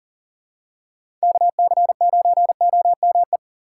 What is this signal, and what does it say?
Morse code, heard as a keyed tone, about 40 wpm, 700 Hz. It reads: KC9OME
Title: USA Call Signs - 6 Characters - 3x Character Spacing - 40wpm